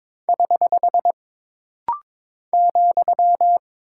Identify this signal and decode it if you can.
Morse code, heard as a keyed tone, about 22 wpm, 700 Hz. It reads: <HH> ,
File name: practice / 22wpm / Single Punctuation-Prosign - Rapid-Fire - 22wpm